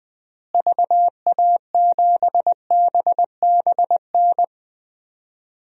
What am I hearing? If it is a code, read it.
Morse code, heard as a keyed tone, about 20 wpm, 700 Hz. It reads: VA7BBN
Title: DXCC-100 Call Signs - 6 Characters - Repeated 3x - 20wpm